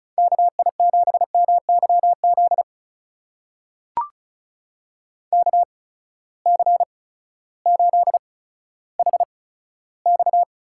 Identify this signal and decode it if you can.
Morse code, heard as a keyed tone, about 34 wpm, spaced as for 16 wpm, 700 Hz. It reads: KI7MYZ KC8HX
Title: USA Call Signs - 6 Characters - 8x Character Spacing - 35wpm